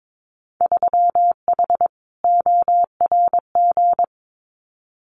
Morse code, heard as a keyed tone, about 22 wpm, 700 Hz. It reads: 3H ORG
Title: Winter Field Day Exchanges Encore 22wpm